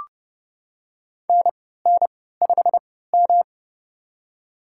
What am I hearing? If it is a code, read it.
Morse code, heard as a keyed tone, about 30 wpm, spaced as for 17 wpm, 700 Hz. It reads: NN5M
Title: USA Call Signs - 4 Characters - 3x Character Spacing - 30wpm